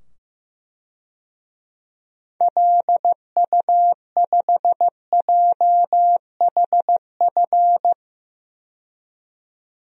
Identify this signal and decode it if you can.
Morse code, heard as a keyed tone, about 15 wpm, 700 Hz. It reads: LU5JHF
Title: DXCC-100 Call Signs - 6 Characters - 15wpm